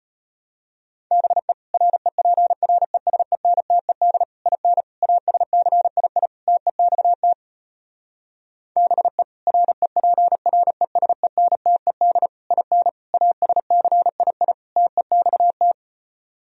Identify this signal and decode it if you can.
Morse code, heard as a keyed tone, about 33 wpm, 700 Hz. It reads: BE REPRESENTED IN ASCII TEXT BE REPRESENTED IN ASCII TEXT